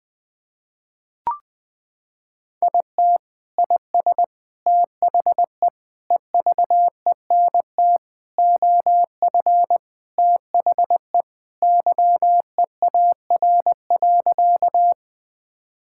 Morse code, heard as a keyed tone, about 20 wpm, 700 Hz. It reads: IT IS THE EVENT OF THE YEAR.
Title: Sentences with 3-Word Phrases - 20wpm